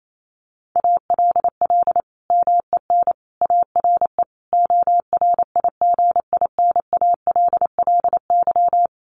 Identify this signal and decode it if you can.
Morse code, heard as a keyed tone, about 28 wpm, 700 Hz. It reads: ALL MEN ARE ORIGINALLY
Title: Sentences from Top 1500 Words 28wpm